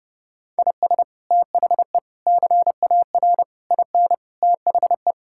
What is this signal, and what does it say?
Morse code, heard as a keyed tone, about 30 wpm, 700 Hz. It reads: IS THE CAR IN THE
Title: Sentences from Top 1600 Words 30wpm